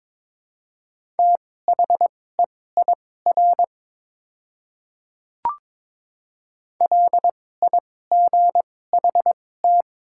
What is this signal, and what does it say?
Morse code, heard as a keyed tone, about 22 wpm, spaced as for 16 wpm, 700 Hz. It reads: THEIR LIGHT
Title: 2x Character Spacing - Top 200 Words - 22wpm